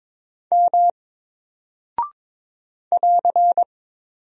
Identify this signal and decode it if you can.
Morse code, heard as a keyed tone, about 22 wpm, 700 Hz. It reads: M <AR>
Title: Single Character (Letter:Number:Punctuation:Prosign) 22wpm